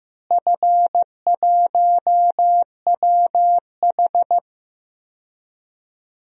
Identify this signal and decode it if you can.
Morse code, heard as a keyed tone, about 15 wpm, 700 Hz. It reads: F1WH
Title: DXCC-100 Call Signs - 4 Characters - Repeated 3x - 15wpm